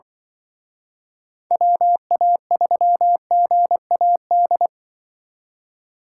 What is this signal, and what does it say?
Morse code, heard as a keyed tone, about 24 wpm, 700 Hz. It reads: WA3GAD
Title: Speed Racing - USA 2x3 Call Signs 20wpm